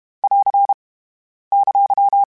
Morse code, heard as a keyed tone, about 32 wpm, 800 Hz. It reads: <AR> !
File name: Contrasts / Contrasts-mp3s / 3LtrContrasts @30